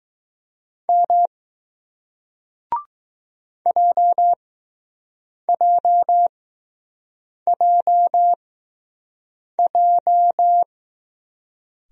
Morse code, heard as a keyed tone, about 21 wpm, spaced as for 6 wpm, 700 Hz. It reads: M JJJJ